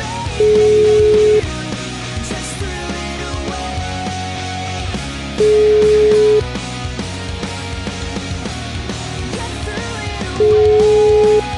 Ringback tone (Europe, 425 Hz), with noise.